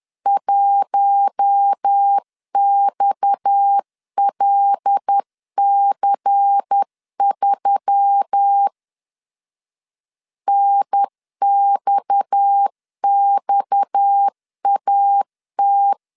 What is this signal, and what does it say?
Morse code, heard as a keyed tone, about 10 wpm, 800 Hz. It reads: 1XLC3 NXXAT
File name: CW / DET / 1 6 1500C S C 10wpm 4252